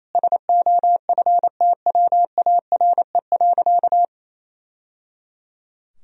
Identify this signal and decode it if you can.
Morse code, heard as a keyed tone, about 28 wpm, 700 Hz. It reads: SOFTWARE.
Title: Sentences from Top 1000 Words - Review Part 3 - 28wpm